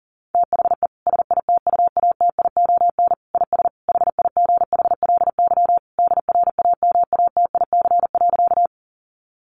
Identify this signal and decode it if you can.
Morse code, heard as a keyed tone, about 40 wpm, 700 Hz. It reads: THE SITUATION IS HIGHLY DRAMATIC.
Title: Sentences from Top 1300 Words 40wpm